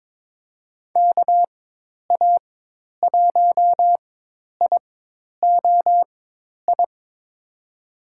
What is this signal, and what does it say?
Morse code, heard as a keyed tone, about 22 wpm, spaced as for 10 wpm, 700 Hz. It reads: KA1IOI
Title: USA Call Signs - 6 Characters - 4x Character Spacing - 22wpm